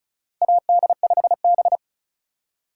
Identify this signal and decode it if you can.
Morse code, heard as a keyed tone, about 36 wpm, 700 Hz. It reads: AD5B
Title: USA Call Signs - 8x Character Spacing - 35wpm